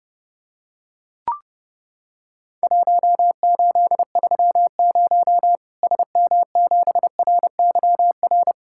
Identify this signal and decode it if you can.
Morse code, heard as a keyed tone, about 30 wpm, 700 Hz. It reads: 1830 SM7RYR
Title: Worked All Europe DX Contest - QTC Exchange - 30wpm